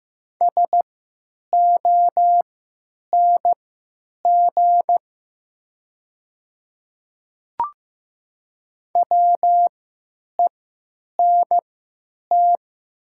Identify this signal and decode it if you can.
Morse code, heard as a keyed tone, about 15 wpm, spaced as for 8 wpm, 700 Hz. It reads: SONG WENT
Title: 3x Character Spacing - Top 500 Words - 15wpm